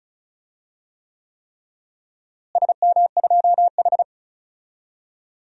Morse code, heard as a keyed tone, about 36 wpm, 700 Hz. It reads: SM2H
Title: DXCC-100 Call Signs - 4 Characters - Repeated 3x - 35wpm